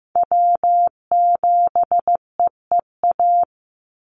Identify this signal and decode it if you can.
Morse code, heard as a keyed tone, about 15 wpm, 700 Hz. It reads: W7EEA
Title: USA Call Signs 15wpm